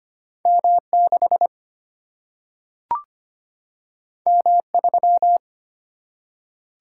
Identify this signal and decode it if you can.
Morse code, heard as a keyed tone, about 25 wpm, 700 Hz. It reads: M 6 M 3
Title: Single-Letter Single-Number 25wpm